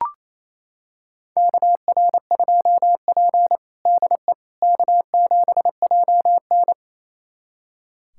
Morse code, heard as a keyed tone, about 28 wpm, 700 Hz. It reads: KR2P DE K7JN
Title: USA Callsign DE Callsign - 4 Characters - 28wpm